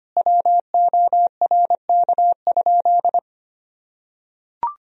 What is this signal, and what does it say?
Morse code, heard as a keyed tone, about 25 wpm, 700 Hz. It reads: WORK?
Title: Sentences from Top 1000 Words - Review Part 4 - 25wpm